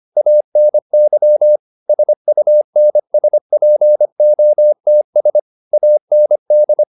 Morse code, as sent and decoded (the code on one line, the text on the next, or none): .- -. -.-- / ... ..- -. ... .--. --- - ... / .- -. -..
ANY SUNSPOTS AND